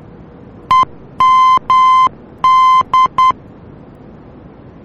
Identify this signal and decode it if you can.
Morse code, heard as a keyed tone, about 10 wpm, 1030 Hz. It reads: EMD
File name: germany / mx / EMD-D